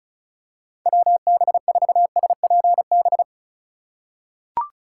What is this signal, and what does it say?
Morse code, heard as a keyed tone, about 34 wpm, 700 Hz. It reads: WB4SPB
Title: K1USN SST QSO Practice - 35wpm